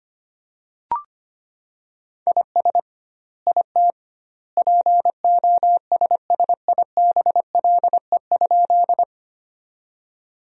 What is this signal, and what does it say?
Morse code, heard as a keyed tone, about 25 wpm, 700 Hz. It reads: IS IT POSSIBLE?
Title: 2x Word-Spacing - Sentences from Top 500 Words - 25wpm